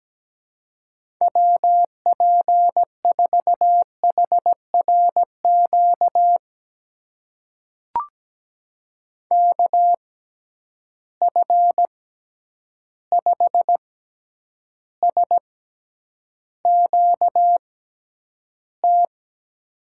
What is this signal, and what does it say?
Morse code, heard as a keyed tone, about 17 wpm, spaced as for 9 wpm, 700 Hz. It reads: WP4HRQ KF5SQT
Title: USA Call Signs - 6 Characters - 6x Character Spacing - 17wpm